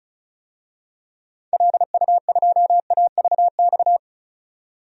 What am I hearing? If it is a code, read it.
Morse code, heard as a keyed tone, about 36 wpm, 700 Hz. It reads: LU2AVX
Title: DXCC-100 Call Signs - 6 Characters - 35wpm